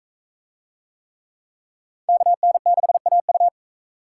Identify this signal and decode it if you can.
Morse code, heard as a keyed tone, about 39 wpm, 700 Hz. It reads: KN6AU